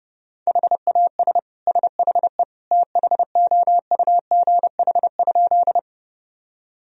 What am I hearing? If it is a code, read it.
Morse code, heard as a keyed tone, about 30 wpm, 700 Hz. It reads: HAS SHE THOUGH?
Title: Sentences from Top 1000 Words - Review Part 2 - 30wpm